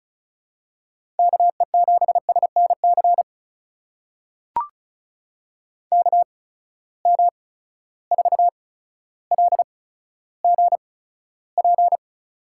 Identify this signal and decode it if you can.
Morse code, heard as a keyed tone, about 36 wpm, spaced as for 15 wpm, 700 Hz. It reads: KE7SNC KM4LGP
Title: USA Call Signs - 6 Characters - 8x Character Spacing - 35wpm